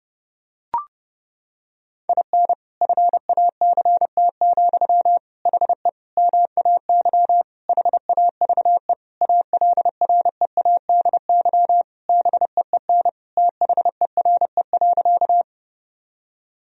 Morse code, heard as a keyed tone, about 31 wpm, 700 Hz. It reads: IN FACT, HE MAY HAVE ALREADY BEEN THERE.
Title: Sentences from Top 1000 Words - Review Part 3 - 30wpm